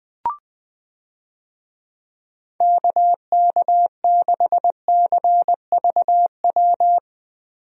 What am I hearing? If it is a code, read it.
Morse code, heard as a keyed tone, about 20 wpm, 700 Hz. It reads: KK6CVW
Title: USA Call Signs - 6 Characters - Repeated 3x - 20wpm